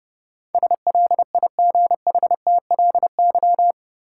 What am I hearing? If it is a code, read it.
Morse code, heard as a keyed tone, about 30 wpm, 700 Hz. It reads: SLIGHTLY
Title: Sentences from Top 1300 Words - New Word Drill - 30wpm